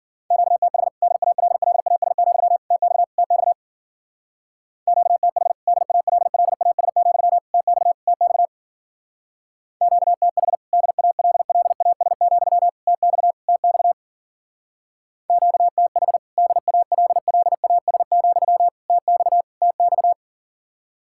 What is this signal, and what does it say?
Morse code, heard as a keyed tone, about 48 wpm, 700 Hz. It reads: QTH DALLAS, TX TX QTH DALLAS, TX TX QTH DALLAS, TX TX QTH DALLAS, TX TX
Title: Speed-Racing Common QSO Phrases - 40wpm